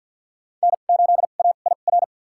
Morse code, heard as a keyed tone, about 52 wpm, spaced as for 36 wpm, 700 Hz. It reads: N8AIR
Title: USA Call Signs - 5 Characters - 2x Character Spacing - 50wpm